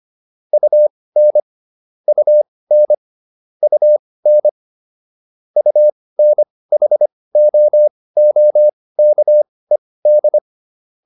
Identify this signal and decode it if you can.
Morse code, heard as a keyed tone, about 25 wpm, spaced as for 18 wpm, 600 Hz. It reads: UN UN UN UNHOOKED